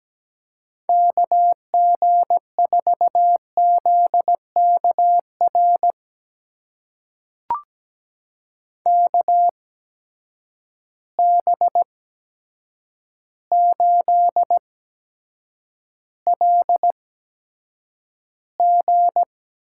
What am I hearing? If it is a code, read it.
Morse code, heard as a keyed tone, about 17 wpm, spaced as for 8 wpm, 700 Hz. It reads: KG4ZKR KB8LG
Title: USA Call Signs - 6 Characters - 8x Character Spacing - 17wpm